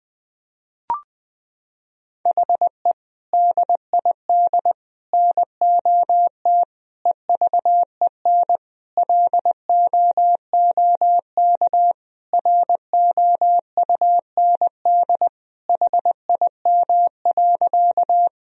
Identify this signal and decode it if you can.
Morse code, heard as a keyed tone, about 21 wpm, 700 Hz. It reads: HE DID NOT EVEN LOOK ROUND HIM.
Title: Sentences from Top 1000 Words - Review Part 1 - 20wpm